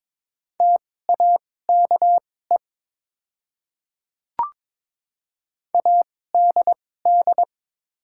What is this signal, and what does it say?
Morse code, heard as a keyed tone, about 22 wpm, spaced as for 16 wpm, 700 Hz. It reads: TAKE ADD
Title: 2x Character Spacing - Top 200 Words - 22wpm